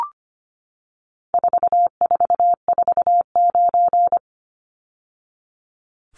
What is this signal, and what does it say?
Morse code, heard as a keyed tone, about 25 wpm, 700 Hz. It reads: 4449
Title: Numbers - 4-Digit 25wpm